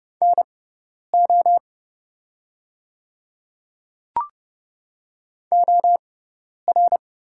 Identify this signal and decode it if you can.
Morse code, heard as a keyed tone, about 30 wpm, spaced as for 10 wpm, 700 Hz. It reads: NO OR
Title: 6x Character Spacing - Top 200 Words - 30wpm